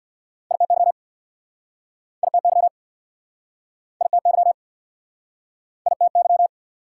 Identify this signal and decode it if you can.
Morse code, heard as a keyed tone, about 69 wpm, 700 Hz. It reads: ITY ITY ITY ITY